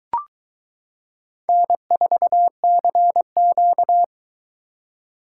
Morse code, heard as a keyed tone, about 22 wpm, 700 Hz. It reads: N4CQ